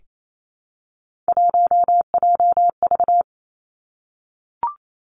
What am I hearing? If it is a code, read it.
Morse code, heard as a keyed tone, about 28 wpm, 700 Hz. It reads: 1JV
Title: Three Challenging Letter-Number Combinations 28wpm